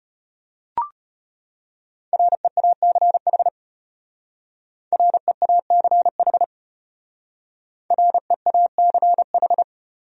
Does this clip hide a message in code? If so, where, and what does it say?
Yes: Morse code from 2.1 to 9.6 s, heard as a keyed tone, about 32 wpm, 700 Hz. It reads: REACH REACH REACH